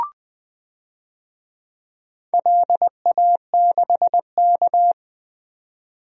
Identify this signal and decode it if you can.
Morse code, heard as a keyed tone, about 20 wpm, 700 Hz. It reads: LA6K
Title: DXCC-100 Call Signs - 4 Characters - 20wpm